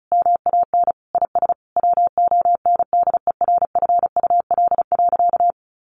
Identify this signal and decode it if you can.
Morse code, heard as a keyed tone, about 34 wpm, 700 Hz. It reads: MAN IS WONDERFUL.
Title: Sentences from Top 1400 Words 35wpm